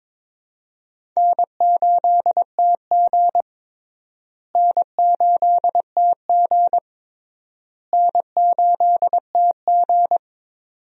Morse code, heard as a keyed tone, about 22 wpm, 700 Hz. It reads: N8TG N8TG N8TG
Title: USA Call Signs - 4 Characters - Repeated 3x - 22wpm